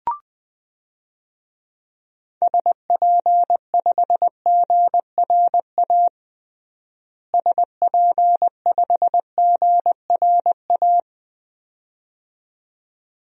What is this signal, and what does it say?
Morse code, heard as a keyed tone, about 20 wpm, 700 Hz. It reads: SP5GRA SP5GRA
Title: DXCC-100 Call Signs - 6 Characters - Repeated 2x - 20wpm